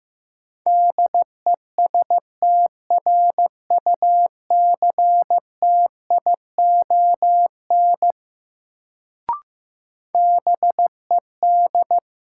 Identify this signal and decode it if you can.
Morse code, heard as a keyed tone, about 15 wpm, 700 Hz. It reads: DESTRUCTION BED
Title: Sets of 5 Words Encore - New Word Drill - 15wpm